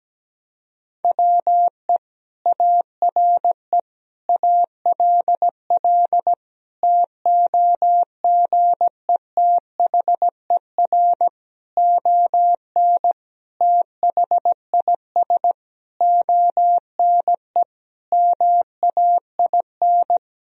Morse code, heard as a keyed tone, about 17 wpm, 700 Hz. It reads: WE ARE ALL TOGETHER ON THIS ONE MAIN
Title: Sentences from Top 1000 Words - Review Part 2 - 17wpm